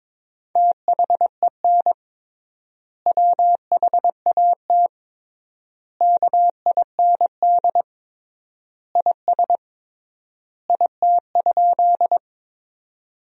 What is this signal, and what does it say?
Morse code, heard as a keyed tone, about 22 wpm, 700 Hz. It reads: THEN WHAT KIND IS IT?
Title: 3x Word-Spacing - Sentences from Top 200 Words - 22wpm